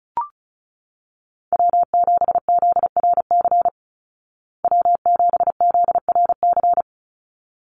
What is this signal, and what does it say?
Morse code, heard as a keyed tone, about 34 wpm, 700 Hz. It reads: W7ZRC W7ZRC
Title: Call Signs Repeated 3x - 35wpm